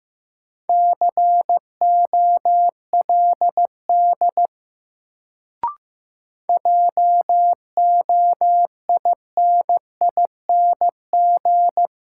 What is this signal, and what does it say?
Morse code, heard as a keyed tone, about 15 wpm, 700 Hz. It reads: COLD JOINING
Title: Sets of 5 Words - New Word Drill - 15wpm